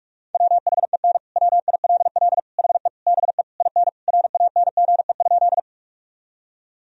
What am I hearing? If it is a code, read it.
Morse code, heard as a keyed tone, about 42 wpm, 700 Hz. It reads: WHEN WILL HE BE IN RANGE?
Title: Sentences from Top 1000 Words - Review Part 4 - 45wpm